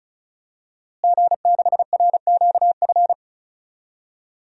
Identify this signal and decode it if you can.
Morse code, heard as a keyed tone, about 35 wpm, 700 Hz. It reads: G6RQF
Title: DXCC-100 Call Signs - 5 Characters - Repeated 3x - 35wpm